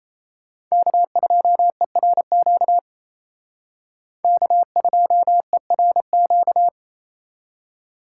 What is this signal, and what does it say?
Morse code, heard as a keyed tone, about 35 wpm, 700 Hz. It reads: K2ERQ K2ERQ